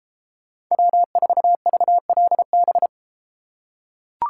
Morse code, heard as a keyed tone, about 33 wpm, 700 Hz. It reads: W4VLB